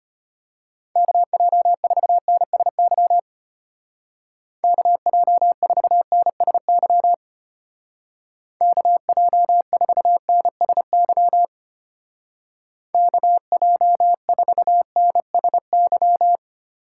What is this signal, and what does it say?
Morse code, heard as a keyed tone, about 32 wpm, 700 Hz. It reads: KJ4NSY KJ4NSY KJ4NSY KJ4NSY